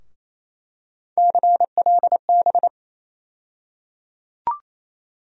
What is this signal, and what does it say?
Morse code, heard as a keyed tone, about 28 wpm, 700 Hz. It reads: CLB
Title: USA Call Sign Postfixes - 3 Letters 28wpm